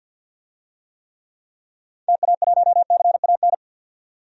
Morse code, heard as a keyed tone, about 50 wpm, 700 Hz. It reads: TA1CAN
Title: DXCC-100 Call Signs - 6 Characters - 50wpm